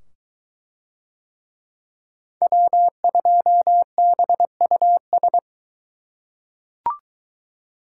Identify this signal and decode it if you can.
Morse code, heard as a keyed tone, about 21 wpm, 700 Hz. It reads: W2BUS